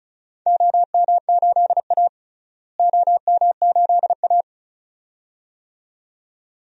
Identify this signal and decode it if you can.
Morse code, heard as a keyed tone, about 34 wpm, 700 Hz. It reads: OM8A OM8A
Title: DXCC-100 Call Signs - 4 Characters - Repeated 2x - 35wpm